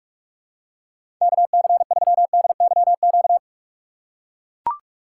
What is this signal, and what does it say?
Morse code, heard as a keyed tone, about 44 wpm, 700 Hz. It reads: KC3DYQ
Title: USA Call Signs - 6 Characters - 6x Character Spacing - 45wpm